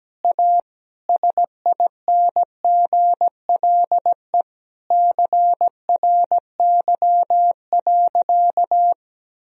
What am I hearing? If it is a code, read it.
Morse code, heard as a keyed tone, about 17 wpm, 700 Hz. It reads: A SINGLE CRY.
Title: Sentences from Top 1000 Words - Review Part 4 - 17wpm